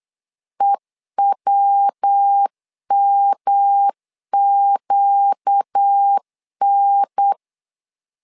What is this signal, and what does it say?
Morse code, heard as a keyed tone, about 8 wpm, 800 Hz. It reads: EWMQN
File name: LEZIONI / 4 Q 8wpm 2732